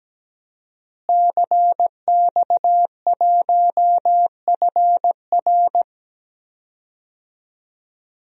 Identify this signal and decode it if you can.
Morse code, heard as a keyed tone, about 17 wpm, 700 Hz. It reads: CX1FR
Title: DXCC-100 Call Signs - 5 Characters - Repeated 2x - 17wpm